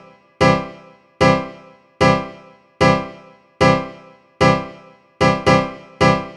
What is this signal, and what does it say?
There is no code.